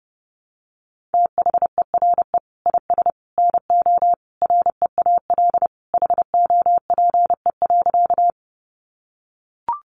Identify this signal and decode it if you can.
Morse code, heard as a keyed tone, about 30 wpm, 700 Hz. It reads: THERE IS NO REAL HOPE.
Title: Sentences from Top 1000 Words 30wpm